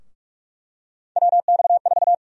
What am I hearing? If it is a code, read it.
Morse code, heard as a keyed tone, about 46 wpm, 700 Hz. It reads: WX4
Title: USA Call Sign Prefixes - 45wpm